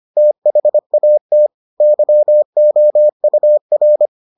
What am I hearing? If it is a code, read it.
Morse code, heard as a keyed tone, about 25 wpm, 600 Hz. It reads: THAT YOUR